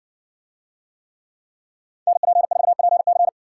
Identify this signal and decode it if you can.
Morse code, heard as a keyed tone, about 60 wpm, 700 Hz. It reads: NP4PC